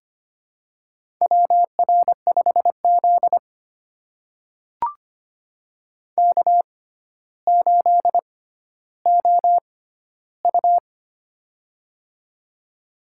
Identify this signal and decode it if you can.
Morse code, heard as a keyed tone, about 25 wpm, spaced as for 13 wpm, 700 Hz. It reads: WR5Z K8OU
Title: USA Call Signs - 4 Characters - 6x Character Spacing - 25wpm